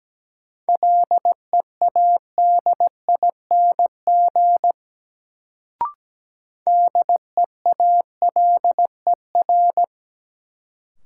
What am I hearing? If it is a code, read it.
Morse code, heard as a keyed tone, about 17 wpm, 700 Hz. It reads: LEADING DEALER
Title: Sentences from Top 1100 Words - New Word Drill - 17wpm